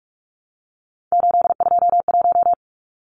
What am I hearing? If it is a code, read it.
Morse code, heard as a keyed tone, about 44 wpm, 700 Hz. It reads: 821
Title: Numbers - 3-Digit 45wpm